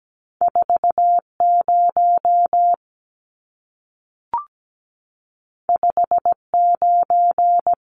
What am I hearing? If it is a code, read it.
Morse code, heard as a keyed tone, about 17 wpm, 700 Hz. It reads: 40 59
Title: Numbers - 2-Digit 17wpm